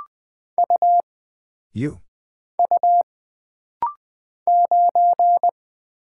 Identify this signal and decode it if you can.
Morse code, heard as a keyed tone, about 20 wpm, 700 Hz. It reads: U U 9
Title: Single Character (Letter:Number:Punctuation:Prosign) - Rapid-Fire - 20wpm